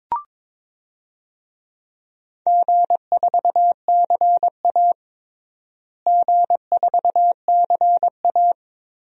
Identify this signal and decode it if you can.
Morse code, heard as a keyed tone, about 21 wpm, 700 Hz. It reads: G4CA G4CA
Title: DXCC-100 Call Signs - 4 Characters - Repeated 2x - 22wpm